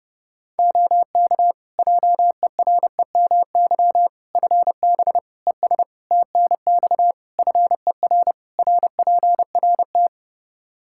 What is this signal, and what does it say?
Morse code, heard as a keyed tone, about 30 wpm, 700 Hz. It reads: OK JEREMY FB ES TNX FER RPRT
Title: Practice QSO - Exchange 2 - Info - 30wpm